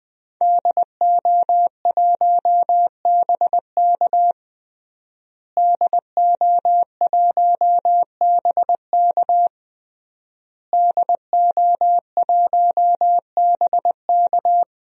Morse code, heard as a keyed tone, about 20 wpm, 700 Hz. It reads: DO1BK DO1BK DO1BK
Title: DXCC-100 Call Signs - 5 Characters - Repeated 3x - 20wpm